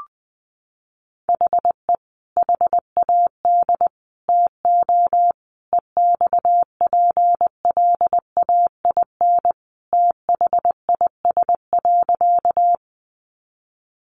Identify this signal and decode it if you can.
Morse code, heard as a keyed tone, about 20 wpm, 700 Hz. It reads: HE HAD TO EXPLAIN THIS.
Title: Sentences from Top 1100 Words 20wpm